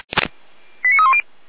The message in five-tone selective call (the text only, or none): 89310